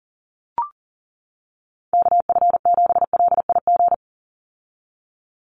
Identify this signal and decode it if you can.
Morse code, heard as a keyed tone, about 40 wpm, 700 Hz. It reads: KF7LIG
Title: USA Call Signs 40wpm